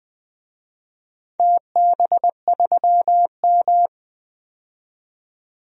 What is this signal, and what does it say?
Morse code, heard as a keyed tone, about 20 wpm, 700 Hz. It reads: TB3M
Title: DXCC-100 Call Signs - 4 Characters - 20wpm